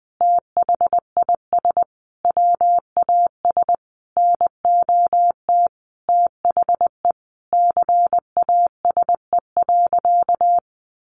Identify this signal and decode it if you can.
Morse code, heard as a keyed tone, about 20 wpm, 700 Hz. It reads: THIS WAS NOT THE CASE.
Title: Sentences from Top 1200 Words 20wpm